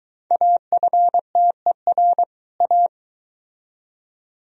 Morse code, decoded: AFTER A

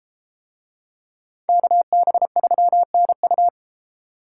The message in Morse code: KB3NU

